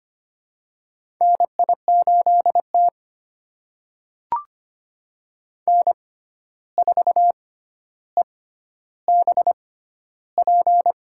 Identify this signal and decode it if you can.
Morse code, heard as a keyed tone, about 25 wpm, spaced as for 12 wpm, 700 Hz. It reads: NI8T N4EBP
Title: USA Call Signs - 6x Character Spacing - 25wpm